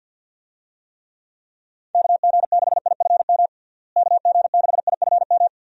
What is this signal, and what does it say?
Morse code, heard as a keyed tone, about 46 wpm, 700 Hz. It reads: KG6IFM KG6IFM
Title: USA Call Signs - 6 Characters - Repeated 2x - 50wpm